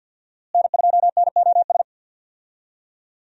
Morse code, heard as a keyed tone, about 52 wpm, 700 Hz. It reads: N2NOS
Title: USA Call Signs - 5 Characters - Repeated 3x - 50wpm